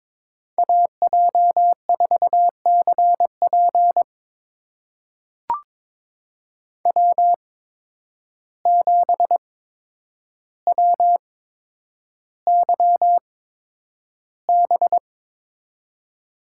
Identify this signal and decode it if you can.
Morse code, heard as a keyed tone, about 22 wpm, spaced as for 9 wpm, 700 Hz. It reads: AJ4CP W7WYB
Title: USA Call Signs - 5 Characters - 8x Character Spacing - 22wpm